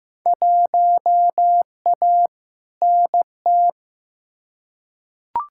Morse code, heard as a keyed tone, about 15 wpm, 700 Hz. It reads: 1A NT
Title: Field Day Exchanges Encore - 15wpm